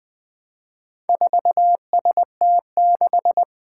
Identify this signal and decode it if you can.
Morse code, heard as a keyed tone, about 20 wpm, 700 Hz. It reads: 4ST6
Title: Four Challenging Letter-Number Combinations 20wpm